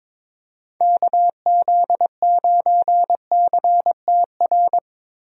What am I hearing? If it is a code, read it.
Morse code, heard as a keyed tone, about 22 wpm, 700 Hz. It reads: KZ9CTR
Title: USA Call Signs - 6 Characters - 22wpm